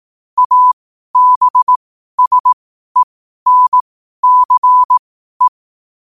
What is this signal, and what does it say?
Morse code, heard as a keyed tone, about 18 wpm, spaced as for 12 wpm, 1000 Hz. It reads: ABSENCE
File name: qotdfiles / 12wpm